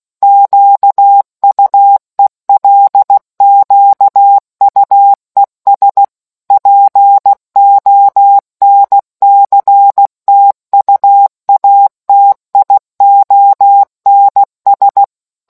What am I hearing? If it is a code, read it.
Morse code, heard as a keyed tone, about 16 wpm, 800 Hz. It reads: QUELQUES PONCTUATIONS